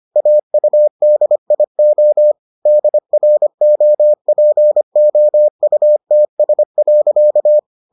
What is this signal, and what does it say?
Morse code, heard as a keyed tone, about 25 wpm, 600 Hz. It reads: AUDIO DROPOUTS.